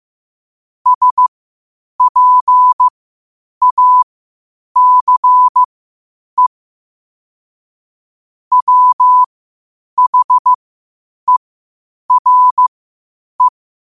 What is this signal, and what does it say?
Morse code, heard as a keyed tone, about 15 wpm, spaced as for 8 wpm, 1000 Hz. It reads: SPACE WHERE
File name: qotdfiles / 8wpm